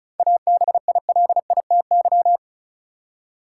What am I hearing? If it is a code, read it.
Morse code, heard as a keyed tone, about 36 wpm, 700 Hz. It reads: ABILITY
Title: Seven Letter Words - 35wpm